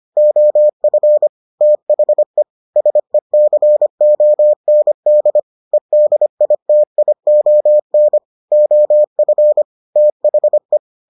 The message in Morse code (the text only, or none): OF THE SECOND EDITION OF THE